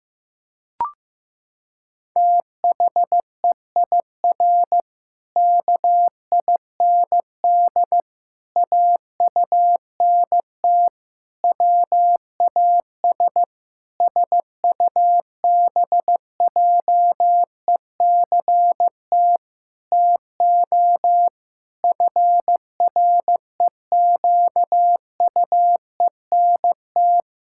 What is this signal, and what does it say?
Morse code, heard as a keyed tone, about 15 wpm, 700 Hz. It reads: THEIR KIND AUNT WAS SUBJECT TO FREQUENT
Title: Pangram Challenge v2 - 15wpm